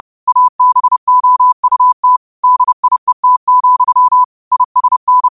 Morse code, heard as a keyed tone, about 30 wpm, 1000 Hz. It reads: ABOUT DIET, ISN